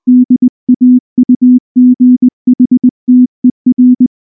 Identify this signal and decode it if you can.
Morse code, heard as a keyed tone, about 19 wpm, 250 Hz. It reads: DAUGHTER
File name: morse / morse-training-20WPM-4